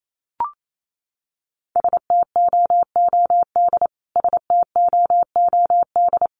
Morse code, heard as a keyed tone, about 28 wpm, 700 Hz. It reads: STOOD STOOD
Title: Top 500 Words Repeated 28wpm